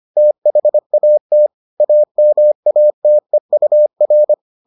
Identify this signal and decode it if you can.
Morse code, heard as a keyed tone, about 25 wpm, 600 Hz. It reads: THAT AMATEUR